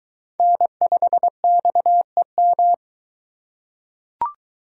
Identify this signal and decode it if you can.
Morse code, heard as a keyed tone, about 20 wpm, 700 Hz. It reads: N5XEM